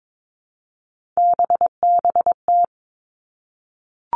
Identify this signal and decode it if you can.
Morse code, heard as a keyed tone, about 22 wpm, 700 Hz. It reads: BBT
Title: Three Challenging Letter-Number Combinations 22wpm